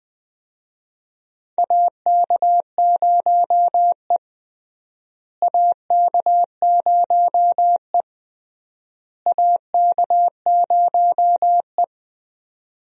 Morse code, heard as a keyed tone, about 20 wpm, 700 Hz. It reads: AK0E AK0E AK0E
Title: USA Call Signs - 4 Characters - Repeated 3x - 20wpm